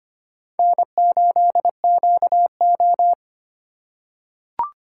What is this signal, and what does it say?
Morse code, heard as a keyed tone, about 25 wpm, 700 Hz. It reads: N8QO
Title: USA Call Signs - 4 Characters - 4x Character Spacing - 25wpm